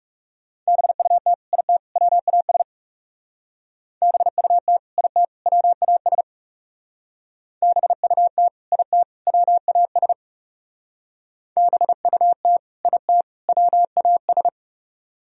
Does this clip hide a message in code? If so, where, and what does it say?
Yes: Morse code from 0.7 to 14.5 s, heard as a keyed tone, about 40 wpm, 700 Hz. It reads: BUT IT WAS BUT IT WAS BUT IT WAS BUT IT WAS